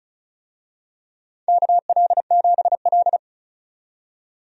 Morse code, heard as a keyed tone, about 36 wpm, 690 Hz. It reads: KL7L
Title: DXCC-100 Call Signs - 4 Characters - 35wpm